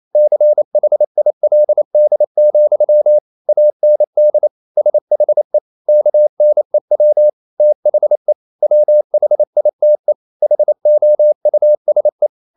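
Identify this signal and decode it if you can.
Morse code, heard as a keyed tone, about 28 wpm, 600 Hz. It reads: CHILD, AND SHE KNEW THE WHITE HOUSE
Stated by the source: Morse code practice at 28 wpm, text of Ken Follett's novel Never in English, from CW med Gnister.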